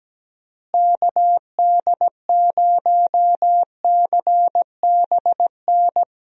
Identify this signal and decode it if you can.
Morse code, heard as a keyed tone, about 17 wpm, 700 Hz. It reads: KD0CBN
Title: USA Call Signs - 6 Characters - Repeated 3x - 17wpm